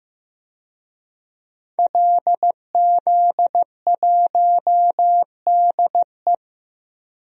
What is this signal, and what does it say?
Morse code, heard as a keyed tone, about 15 wpm, 700 Hz. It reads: LZ1DE
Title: DXCC-100 Call Signs - 5 Characters - 15wpm